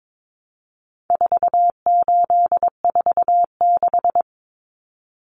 Morse code, heard as a keyed tone, about 22 wpm, 700 Hz. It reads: 4846
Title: Common Cut Numbers - 4-Digit 22wpm